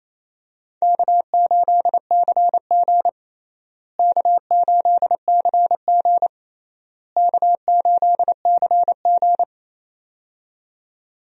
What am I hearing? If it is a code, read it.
Morse code, heard as a keyed tone, about 28 wpm, 700 Hz. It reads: K8CG K8CG K8CG
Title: USA Call Signs - 4 Characters - Repeated 3x - 28wpm